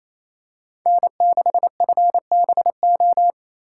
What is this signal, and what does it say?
Morse code, heard as a keyed tone, about 28 wpm, 700 Hz. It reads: N6FBO
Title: USA Call Signs - 5 Characters - 2x Character Spacing - 28wpm